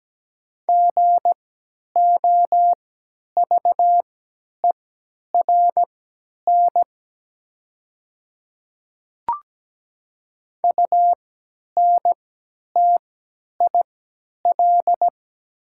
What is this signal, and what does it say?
Morse code, heard as a keyed tone, about 17 wpm, spaced as for 10 wpm, 690 Hz. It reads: GOVERN UNTIL
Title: 3x Character Spacing - Top 500 Words - 17wpm